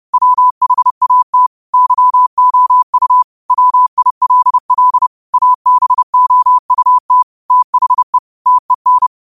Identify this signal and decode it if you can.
Morse code, heard as a keyed tone, about 30 wpm, 1000 Hz. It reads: WHAT YOU WILL ABOUT THE TEN